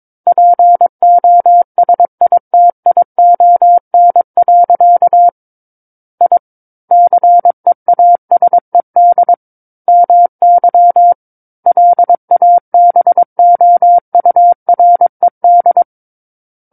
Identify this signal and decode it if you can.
Morse code, heard as a keyed tone, about 22 wpm, 700 Hz. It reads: POSITION. I CEASED MY LABOURED
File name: Book One / 13-Chapter 13 - 22 WPM